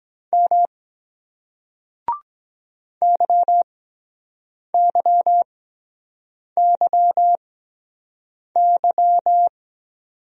Morse code, heard as a keyed tone, about 23 wpm, spaced as for 7 wpm, 700 Hz. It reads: M YYYY